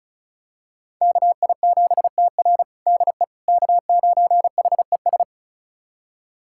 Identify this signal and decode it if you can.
Morse code, heard as a keyed tone, about 34 wpm, 700 Hz. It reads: KI7TR DE K9HES
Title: USA Callsign DE Callsign - 5 Characters - 35wpm